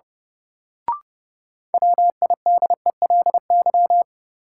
Morse code, heard as a keyed tone, about 32 wpm, 700 Hz. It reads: WIDELY
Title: Sentences from Top 1200 Words - New Word Drill Alt - 30wpm